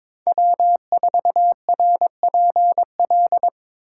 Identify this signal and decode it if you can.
Morse code, heard as a keyed tone, about 26 wpm, 700 Hz. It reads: W4RPL
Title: Speed Racing - USA 1x3 Call Signs 22wpm